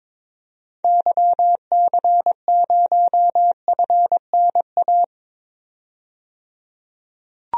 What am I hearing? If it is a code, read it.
Morse code, heard as a keyed tone, about 21 wpm, 700 Hz. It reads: YC0FNA
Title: DXCC-100 Call Signs - 6 Characters - 22wpm